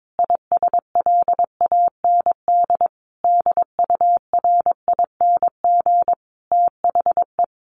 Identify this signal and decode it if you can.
Morse code, heard as a keyed tone, about 22 wpm, 700 Hz. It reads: ISLAND DURING THE
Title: Sentences from Top 1400 Words 22wpm